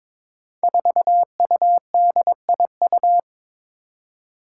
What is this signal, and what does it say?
Morse code, heard as a keyed tone, about 22 wpm, 700 Hz. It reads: 4UDIU
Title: Five Challenging Letter-Number Combinations 22wpm